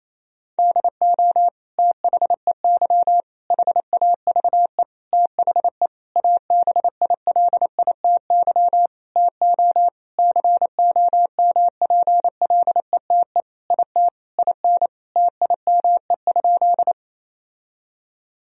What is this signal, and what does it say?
Morse code, heard as a keyed tone, about 27 wpm, 700 Hz. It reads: DO THEY HAVE THE ABILITY TO COMPLETE IT IN TIME?
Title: Sentences from Top 1000 Words - Review Part 3 - 28wpm